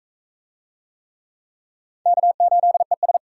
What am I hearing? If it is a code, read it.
Morse code, heard as a keyed tone, about 44 wpm, 700 Hz. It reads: K8ES